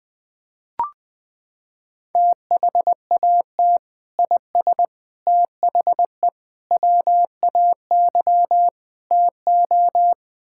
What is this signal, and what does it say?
Morse code, heard as a keyed tone, about 20 wpm, 700 Hz. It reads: THAT IS THE WAY TO
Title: Sentences with 3-Word Phrases - 20wpm